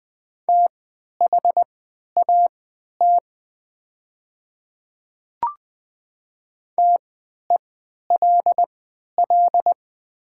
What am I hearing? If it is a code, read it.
Morse code, heard as a keyed tone, about 20 wpm, spaced as for 11 wpm, 700 Hz. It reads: THAT TELL
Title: 3x Character Spacing - Top 200 Words - 20wpm